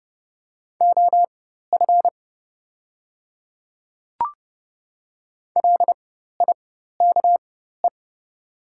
Morse code, heard as a keyed tone, about 31 wpm, spaced as for 14 wpm, 700 Hz. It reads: OF LIKE
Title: 4x Character Spacing - Top 100 Words - 30wpm